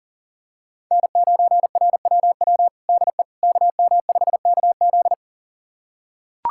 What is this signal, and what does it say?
Morse code, heard as a keyed tone, about 40 wpm, 700 Hz. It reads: N9RWW DE KM5KZ
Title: USA Callsign DE Callsign - 5 Characters - 40wpm